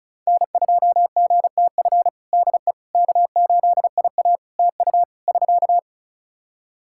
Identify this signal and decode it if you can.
Morse code, heard as a keyed tone, about 34 wpm, 700 Hz. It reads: N2GTF DE K8IA TU <SK>
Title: Practice QSO - Exchange 4 - Ending - 35wpm